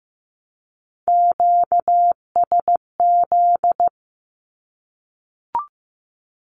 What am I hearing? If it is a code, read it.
Morse code, heard as a keyed tone, about 15 wpm, 700 Hz. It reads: QSZ
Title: Q-Codes Repeated 3x 15wpm